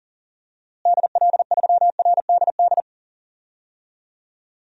Morse code, heard as a keyed tone, about 40 wpm, 700 Hz. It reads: DL3RDD
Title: DXCC-100 Call Signs - 6 Characters - Repeated 3x - 40wpm